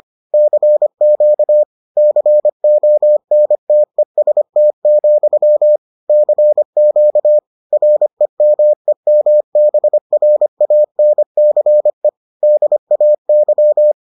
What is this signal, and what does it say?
Morse code, heard as a keyed tone, about 25 wpm, 600 Hz. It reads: CQ CONTEST, CQ REMEMBRANCE DAY